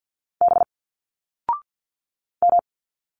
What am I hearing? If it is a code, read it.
Morse code, heard as a keyed tone, about 50 wpm, 700 Hz. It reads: B M
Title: Single Letters 50wpm